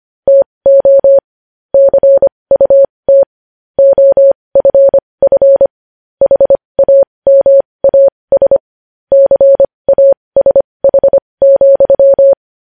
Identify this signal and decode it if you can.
Morse code, heard as a keyed tone, about 25 wpm, spaced as for 20 wpm, 550 Hz. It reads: TO CUT OFF HAMAS CASH,